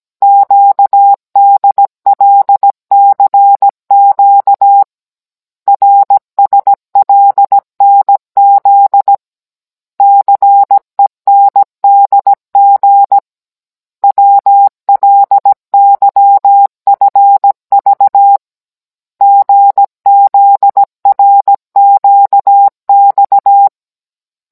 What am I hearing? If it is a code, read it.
Morse code, heard as a keyed tone, about 17 wpm, 800 Hz. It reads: QDLCQ RSLNZ CENDG WLYFV GZRQX